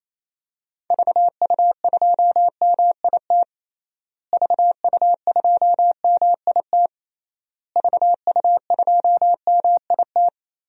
Morse code, heard as a keyed tone, about 28 wpm, 700 Hz. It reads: VU2MIT VU2MIT VU2MIT
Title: DXCC-100 Call Signs - 6 Characters - Repeated 3x - 28wpm